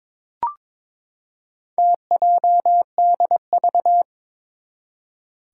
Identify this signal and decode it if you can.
Morse code, heard as a keyed tone, about 22 wpm, 700 Hz. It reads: TJDV
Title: Four Challenging Letter-Number Combinations 22wpm